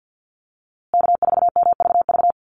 Morse code, heard as a keyed tone, about 50 wpm, 700 Hz. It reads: K4MVV
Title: USA Call Signs Repeated 3x 50wpm